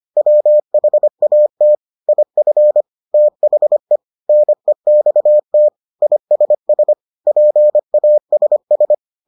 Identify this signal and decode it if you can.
Morse code, heard as a keyed tone, about 25 wpm, 600 Hz. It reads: WHAT IF THE NEXT ISS PASS